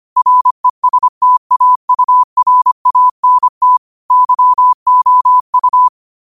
Morse code, heard as a keyed tone, about 25 wpm, 1000 Hz. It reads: RESTAURANT YOU
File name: qotdfiles / 25wpm